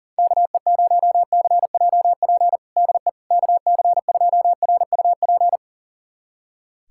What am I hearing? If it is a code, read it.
Morse code, heard as a keyed tone, about 40 wpm, 700 Hz. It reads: KE0CJP DE KC2RUP
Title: USA Callsign DE Callsign - 6 Characters - 40wpm